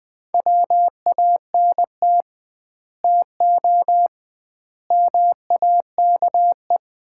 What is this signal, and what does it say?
Morse code, heard as a keyed tone, about 20 wpm, 700 Hz. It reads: WANT TO MAKE
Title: 2x Word-Spacing - Sets of 5 Words - 20wpm